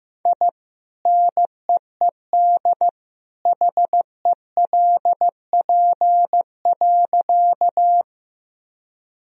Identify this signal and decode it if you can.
Morse code, heard as a keyed tone, about 16 wpm, 700 Hz. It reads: I NEED HELP.
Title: Sentences from Top 1000 Words - Review Part 1 - 15wpm